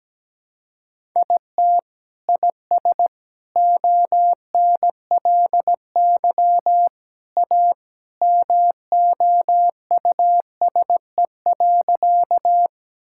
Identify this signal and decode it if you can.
Morse code, heard as a keyed tone, about 17 wpm, 700 Hz. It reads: IT IS ONLY A MOUSE.
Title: Sentences from Top 1000 Words - Review Part 5 - 17wpm